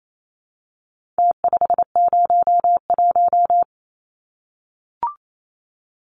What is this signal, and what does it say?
Morse code, heard as a keyed tone, about 28 wpm, 700 Hz. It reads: T501
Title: Numbers - Contest Style - 4-Digit 28wpm